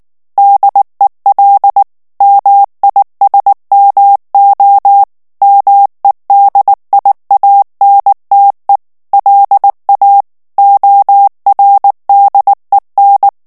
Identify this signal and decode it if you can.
Morse code, heard as a keyed tone, about 19 wpm, 800 Hz. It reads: DEL MISMO MEDIANTE LA ORDEN
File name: morse mp3 / ESPAGNOL TEXTE EN CLAIR 2 VITESSE 18 MOTS MINUTES